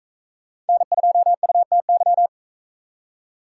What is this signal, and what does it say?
Morse code, heard as a keyed tone, about 43 wpm, 700 Hz. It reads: N2UTY